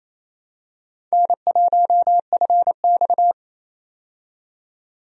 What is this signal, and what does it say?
Morse code, heard as a keyed tone, about 28 wpm, 700 Hz. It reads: N1FX
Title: USA Call Signs - 4 Characters - 28wpm